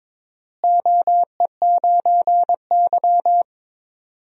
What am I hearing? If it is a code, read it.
Morse code, heard as a keyed tone, about 22 wpm, 700 Hz. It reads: OE9Y
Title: DXCC-100 Call Signs - 4 Characters - Repeated 2x - 22wpm